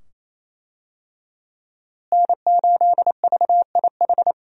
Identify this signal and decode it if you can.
Morse code, heard as a keyed tone, about 28 wpm, 700 Hz. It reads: N8VIH
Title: USA Call Signs - 5 Characters - Repeated 2x - 28wpm